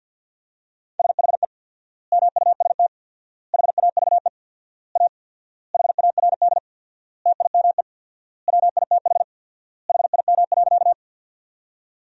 Morse code, heard as a keyed tone, about 53 wpm, 700 Hz. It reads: SHE MUST HAVE A HARD TIME WITH HIM.
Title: 4x Word-Spacing - Sentences from Top 300 Words - 50wpm